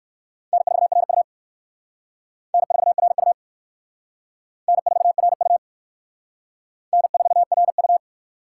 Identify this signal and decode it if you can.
Morse code, heard as a keyed tone, about 55 wpm, 700 Hz. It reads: N4RU N4RU N4RU N4RU
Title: Speed Racing - USA 1x2 Call Signs 45wpm